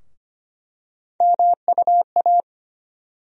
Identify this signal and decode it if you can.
Morse code, heard as a keyed tone, about 25 wpm, 700 Hz. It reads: MUA
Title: USA Call Sign Postfixes - 3 Letters 25wpm